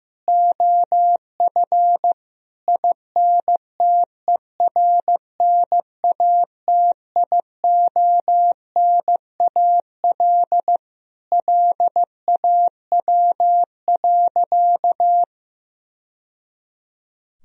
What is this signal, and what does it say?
Morse code, heard as a keyed tone, about 15 wpm, 700 Hz. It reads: OF INTERNATIONAL LAW.
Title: Sentences from Top 1000 Words - Review Part 4 - 15wpm